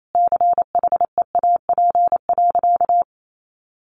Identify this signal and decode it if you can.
Morse code, heard as a keyed tone, about 28 wpm, 700 Hz. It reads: CHEAP.
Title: Sentences from Top 1200 Words 28wpm